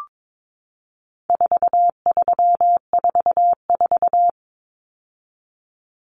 Morse code, heard as a keyed tone, about 22 wpm, 700 Hz. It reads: 4344
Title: Common Cut Numbers - 4-Digit 22wpm